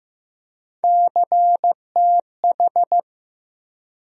Morse code, heard as a keyed tone, about 15 wpm, 700 Hz. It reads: CTH
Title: USA Call Sign Postfixes - 3 Letters 15wpm